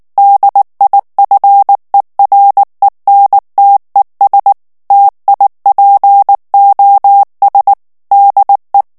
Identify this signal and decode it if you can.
Morse code, heard as a keyed tone, about 19 wpm, 800 Hz. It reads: DIFERENTES TIPOS DE
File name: morse mp3 / ESPAGNOL TEXTE EN CLAIR 2 VITESSE 18 MOTS MINUTES